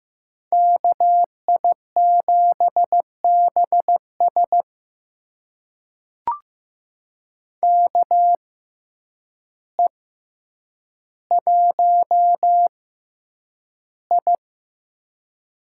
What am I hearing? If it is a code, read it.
Morse code, heard as a keyed tone, about 15 wpm, spaced as for 8 wpm, 700 Hz. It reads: KI7BS KE1I
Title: USA Call Signs - 5 Characters - 6x Character Spacing - 15wpm